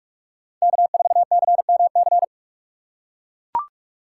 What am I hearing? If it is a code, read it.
Morse code, heard as a keyed tone, about 41 wpm, 700 Hz. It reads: K4CMC